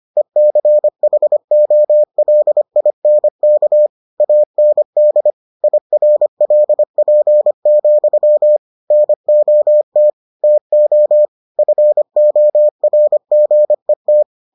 Morse code, heard as a keyed tone, about 25 wpm, 600 Hz. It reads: ECHOLINK AND IRLP, NOT TO FORGET